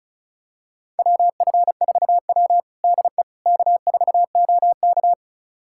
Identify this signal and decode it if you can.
Morse code, heard as a keyed tone, about 36 wpm, 700 Hz. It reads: WF4W DE K4OK
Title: USA Callsign DE Callsign - 4 Characters - 35wpm